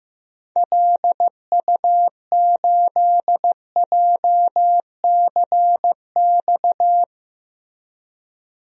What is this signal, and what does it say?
Morse code, heard as a keyed tone, about 15 wpm, 700 Hz. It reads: LU8JCX
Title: DXCC-100 Call Signs - 6 Characters - Repeated 3x - 15wpm